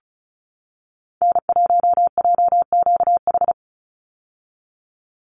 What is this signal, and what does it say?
Morse code, heard as a keyed tone, about 34 wpm, 700 Hz. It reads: N1JQH
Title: USA Call Signs 35wpm